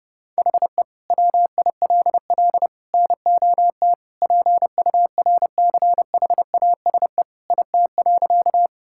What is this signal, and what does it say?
Morse code, heard as a keyed tone, about 30 wpm, 700 Hz. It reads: HE WILL NOT PURCHASE IT.
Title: Sentences from Top 1000 Words - Review Part 5 - 30wpm